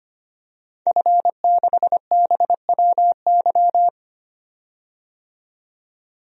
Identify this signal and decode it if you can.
Morse code, heard as a keyed tone, about 25 wpm, 700 Hz. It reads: F6BWY
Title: DXCC-100 Call Signs - 5 Characters - 25wpm